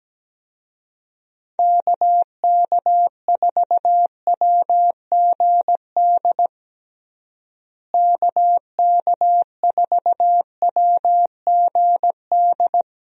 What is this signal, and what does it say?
Morse code, heard as a keyed tone, about 17 wpm, 700 Hz. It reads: KK4WGD KK4WGD
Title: USA Call Signs - 6 Characters - Repeated 3x - 17wpm